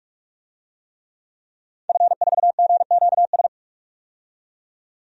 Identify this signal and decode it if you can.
Morse code, heard as a keyed tone, about 45 wpm, 700 Hz. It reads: F4GQS
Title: DXCC-100 Call Signs - 5 Characters - 45wpm